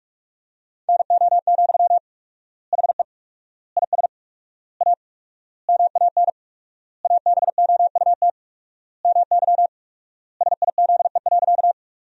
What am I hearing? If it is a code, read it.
Morse code, heard as a keyed tone, about 41 wpm, 700 Hz. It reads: NO, HE IS A MAN ABOUT MY SIZE.
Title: 4x Word-Spacing - Sentences from Top 400 Words - 45wpm